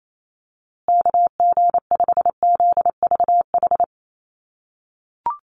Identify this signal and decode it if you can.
Morse code, heard as a keyed tone, about 28 wpm, 700 Hz. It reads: KG5ZVH
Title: USA Call Signs Repeated 2x 28wpm